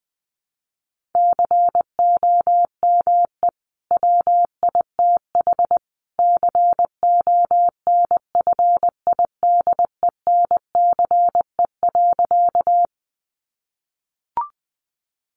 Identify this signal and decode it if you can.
Morse code, heard as a keyed tone, about 20 wpm, 700 Hz. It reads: COME WITH CONFIDENCE.
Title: Sentences from Top 1500 Words 20wpm